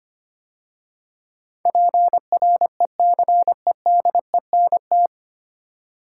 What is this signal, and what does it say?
Morse code, heard as a keyed tone, about 25 wpm, 700 Hz. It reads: PRECEDENT
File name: practice / 25wpm / 9 Letter Words - First 3 Letters Given - 25wpm